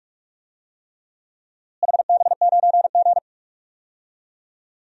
Morse code, heard as a keyed tone, about 46 wpm, 700 Hz. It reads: HB9G
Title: DXCC-100 Call Signs - 4 Characters - Repeated 3x - 45wpm